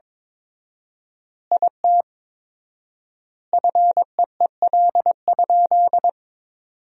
Morse code, heard as a keyed tone, about 22 wpm, 700 Hz. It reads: IT FEEL?
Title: 4x Word-Spacing - Sentences from Top 400 Words - 22wpm